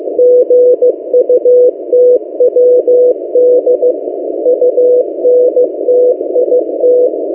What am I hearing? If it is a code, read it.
Morse code, heard as a keyed tone, about 15 wpm, 500 Hz. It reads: GUTWD UNTIT
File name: M08a- -1355utc-8096khz